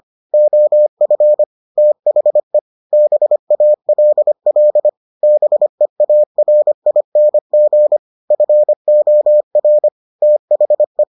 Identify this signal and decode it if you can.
Morse code, heard as a keyed tone, about 25 wpm, 600 Hz. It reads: OF THE BALL BEARING FOR THE